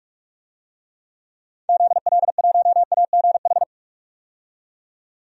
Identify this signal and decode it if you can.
Morse code, heard as a keyed tone, about 46 wpm, 700 Hz. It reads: ZL1AGH
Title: DXCC-100 Call Signs - 6 Characters - 45wpm